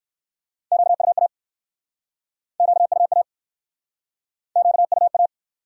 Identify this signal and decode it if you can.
Morse code, heard as a keyed tone, about 62 wpm, 700 Hz. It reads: QUA QUA QUA